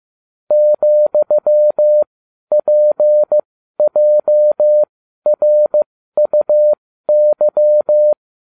Morse code, heard as a keyed tone, about 15 wpm, spaced as for 12 wpm, 600 Hz. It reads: ,PJRUY